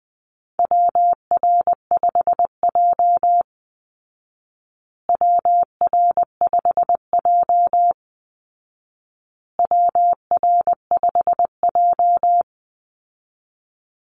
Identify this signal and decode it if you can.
Morse code, heard as a keyed tone, about 20 wpm, 700 Hz. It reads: WR5J WR5J WR5J
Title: Call Signs Repeated 2x - 20wpm